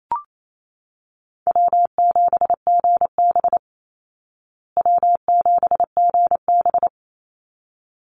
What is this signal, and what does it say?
Morse code, heard as a keyed tone, about 28 wpm, 700 Hz. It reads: W7GB W7GB
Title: Easy Call Signs Repeated 2x - 28wpm